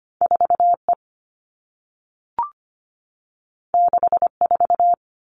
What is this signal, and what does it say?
Morse code, heard as a keyed tone, about 25 wpm, 700 Hz. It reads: 4E 64
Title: All Cut Numbers - 2-Digit 25wpm